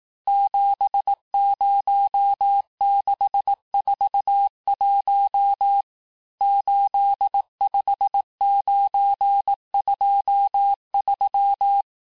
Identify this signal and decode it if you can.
Morse code, heard as a keyed tone, about 18 wpm, 800 Hz. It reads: 70641 85923
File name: cours-du-lundi files / cours-1852-texte-chiffre-vitesse-17MPM